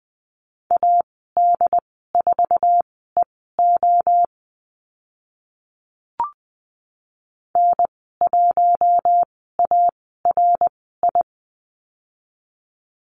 Morse code, heard as a keyed tone, about 20 wpm, spaced as for 14 wpm, 700 Hz. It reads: AD4EO N1ARI
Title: 069 Lesson 12g - Letter d - Call Signs 20wpm